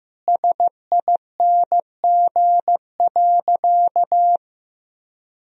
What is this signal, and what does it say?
Morse code, heard as a keyed tone, about 15 wpm, 700 Hz. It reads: SING.
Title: Sentences from Top 1000 Words - Review Part 2 - 15wpm